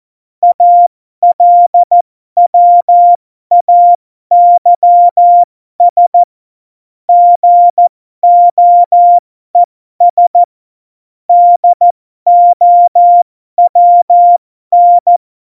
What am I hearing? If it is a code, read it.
Morse code, heard as a keyed tone, about 14 wpm, spaced as for 12 wpm, 700 Hz. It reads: ALWAYS GOES DOWN
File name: Morsekurs / Witze